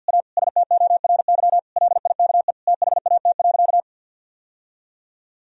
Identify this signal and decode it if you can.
Morse code, heard as a keyed tone, about 46 wpm, 700 Hz. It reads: A STORY LIKE THAT.
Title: Sentences from Top 1000 Words - Review Part 3 - 50wpm